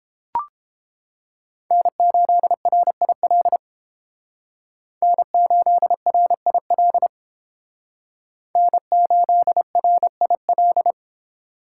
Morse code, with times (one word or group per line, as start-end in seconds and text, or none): 1.7-3.6 N8RIL
5.0-7.1 N8RIL
8.6-10.9 N8RIL